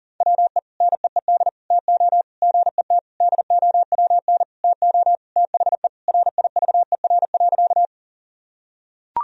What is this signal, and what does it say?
Morse code, heard as a keyed tone, about 37 wpm, 700 Hz. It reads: WE NEED TO GET DOWN TO THE RIVER.
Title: Sentences from Top 1000 Words - Review Part 1 - 40wpm